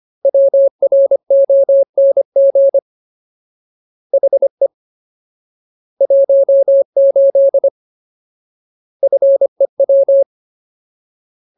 Morse code, heard as a keyed tone, about 25 wpm, 550 Hz. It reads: WRONG HE 18 FEW
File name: Ultimate Word List / 25WPM Ultimate